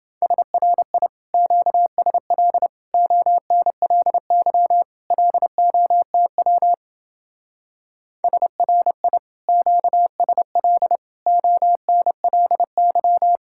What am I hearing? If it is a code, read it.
Morse code, heard as a keyed tone, about 28 wpm, 700 Hz. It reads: SRI QSL ONLY LOTW SRI QSL ONLY